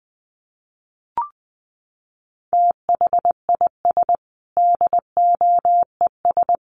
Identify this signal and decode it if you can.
Morse code, heard as a keyed tone, about 20 wpm, 700 Hz. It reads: THIS DOES
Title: Sets of 5 Words Encore2 20wpm